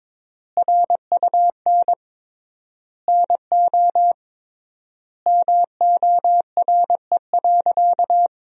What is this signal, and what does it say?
Morse code, heard as a keyed tone, about 22 wpm, 700 Hz. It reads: RUN NO MORE.
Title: 3x Word-Spacing - Sentences from Top 300 Words - 22wpm